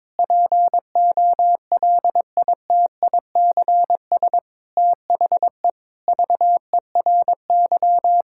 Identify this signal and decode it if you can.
Morse code, heard as a keyed tone, about 22 wpm, 700 Hz. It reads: POLITICS THE VERY
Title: Sentences from Top 1000 Words - Review Part 6 - 22wpm